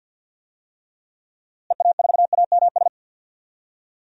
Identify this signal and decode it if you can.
Morse code, heard as a keyed tone, about 46 wpm, 700 Hz. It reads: EA4AMS
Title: DXCC-100 Call Signs - 6 Characters - Repeated 3x - 50wpm